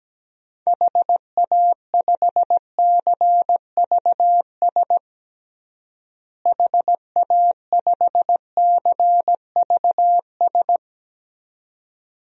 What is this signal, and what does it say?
Morse code, heard as a keyed tone, about 17 wpm, 700 Hz. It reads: HA5CVS HA5CVS
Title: DXCC-100 Call Signs - 6 Characters - Repeated 2x - 17wpm